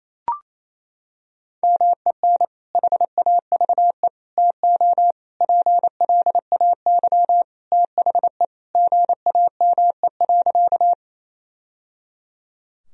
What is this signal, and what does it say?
Morse code, heard as a keyed tone, about 28 wpm, 700 Hz. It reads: MEN HAVE TO PLAY THE GAME.
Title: Sentences from Top 1000 Words - Review Part 3 - 28wpm